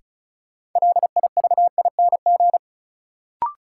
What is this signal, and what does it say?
Morse code, heard as a keyed tone, about 36 wpm, 700 Hz. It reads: LIVING
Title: Sentences from Top 1000 Words - Review Part 6 - New Word Drill - 35wpm